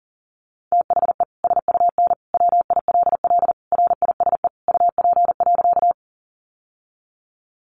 Morse code, heard as a keyed tone, about 40 wpm, 700 Hz. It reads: THE SUN WILL RISE UP.
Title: Sentences from Top 900 Words 40wpm